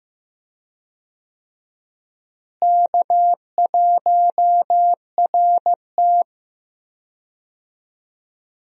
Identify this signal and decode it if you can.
Morse code, heard as a keyed tone, about 15 wpm, 700 Hz. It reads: K1RT
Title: USA Call Signs - 4 Characters - Repeated 3x - 15wpm